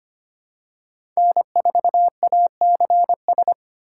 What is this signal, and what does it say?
Morse code, heard as a keyed tone, about 25 wpm, 700 Hz. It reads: N4ACS
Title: USA Call Signs - 5 Characters - 3x Character Spacing - 25wpm